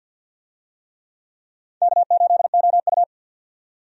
Morse code, heard as a keyed tone, about 50 wpm, 700 Hz. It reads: K8OU
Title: USA Call Signs - 4 Characters - 50wpm